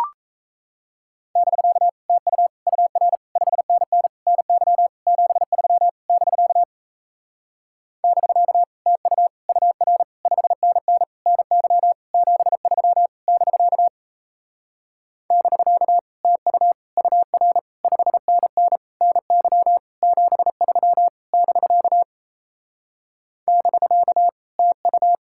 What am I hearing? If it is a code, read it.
Morse code, heard as a keyed tone, about 37 wpm, 700 Hz. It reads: <BK> TU UR 5NN NY 73 <BK> <BK> TU UR 5NN NY 73 <BK> <BK> TU UR 5NN NY 73 <BK> <BK> TU